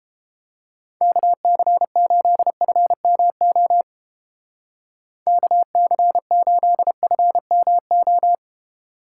Morse code, heard as a keyed tone, about 30 wpm, 700 Hz. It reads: KC8FMO KC8FMO